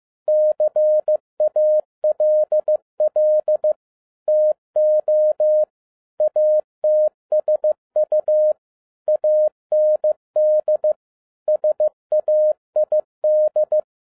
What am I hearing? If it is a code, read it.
Morse code, heard as a keyed tone, about 15 wpm, 610 Hz. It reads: CALL TO ATSU AND SAID
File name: 2011-07-mp3 / n47 JOHN K3TN 15wpm